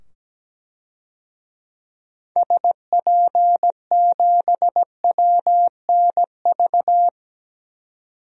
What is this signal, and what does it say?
Morse code, heard as a keyed tone, about 17 wpm, 700 Hz. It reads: SP7WNV
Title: DXCC-100 Call Signs - 6 Characters - 17wpm